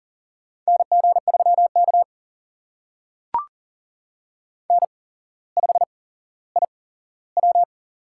Morse code, heard as a keyed tone, about 40 wpm, spaced as for 17 wpm, 700 Hz. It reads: NG3K N5IW
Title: USA Call Signs - 4 Characters - 8x Character Spacing - 40wpm